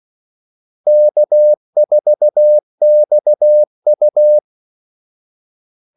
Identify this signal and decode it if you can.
Morse code, heard as a keyed tone, about 16 wpm, 600 Hz. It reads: K4XU
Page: Kaldesignaler 16 wpm